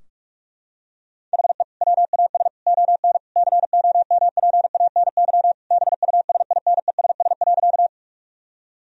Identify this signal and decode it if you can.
Morse code, heard as a keyed tone, about 44 wpm, 700 Hz. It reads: HE WAS ON COMPANY BUSINESS.
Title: Sentences from Top 1000 Words - Review Part 3 - 45wpm